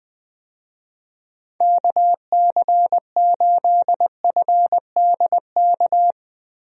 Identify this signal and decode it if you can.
Morse code, heard as a keyed tone, about 20 wpm, 700 Hz. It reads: KC8FDK
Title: USA Call Signs - 6 Characters - Repeated 3x - 20wpm